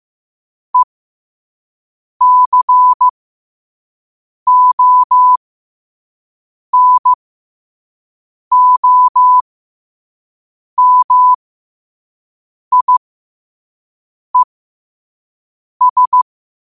Morse code, heard as a keyed tone, about 15 wpm, spaced as for 5 wpm, 1000 Hz. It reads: ECONOMIES
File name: qotdfiles / 5wpm